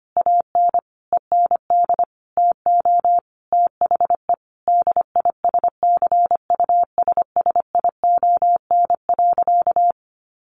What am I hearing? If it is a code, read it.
Morse code, heard as a keyed tone, about 25 wpm, 700 Hz. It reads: AN END TO THE DISCUSSION.
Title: Sentences from Top 1100 Words 25wpm